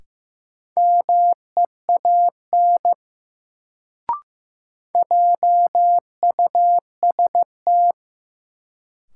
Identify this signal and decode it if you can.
Morse code, heard as a keyed tone, about 15 wpm, 700 Hz. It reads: MEAN JUST
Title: Sets of 2 Words - New Word Drill - 15wpm